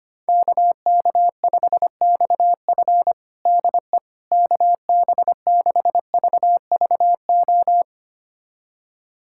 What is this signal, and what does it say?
Morse code, heard as a keyed tone, about 25 wpm, 700 Hz. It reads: KK5XF DE KB6VVO
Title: USA Callsign DE Callsign 25wpm